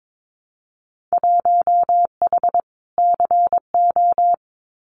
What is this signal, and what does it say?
Morse code, heard as a keyed tone, about 22 wpm, 700 Hz. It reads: 1H CO
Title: Winter Field Day Exchanges 22wpm